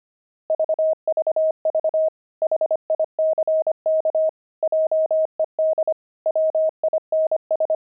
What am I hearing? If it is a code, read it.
Morse code, heard as a keyed tone, about 25 wpm, 630 Hz. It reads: VVV HICK JED WINS